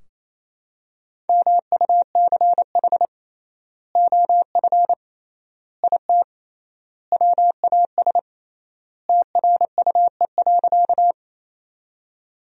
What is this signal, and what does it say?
Morse code, heard as a keyed tone, about 28 wpm, 700 Hz. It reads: MUCH OF IT WAS TRUE.
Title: 3x Word-Spacing - Sentences from Top 400 Words - 28wpm